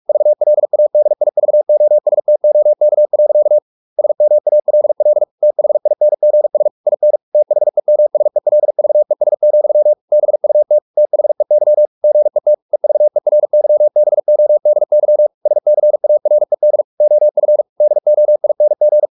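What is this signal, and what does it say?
Morse code, heard as a keyed tone, about 44 wpm, 600 Hz. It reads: VLADIVOSTOK. SMALL THINGS IN THEMSELVES, BUT THEY GET EVERYBODY SCARED OF DOING
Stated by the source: Morse practice at 45 wpm, the novel Never by Ken Follett in English, from CW med Gnister.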